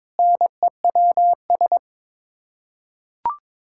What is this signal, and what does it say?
Morse code, heard as a keyed tone, about 22 wpm, 700 Hz. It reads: NEWS
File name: Sets of 9 Words - 22wpm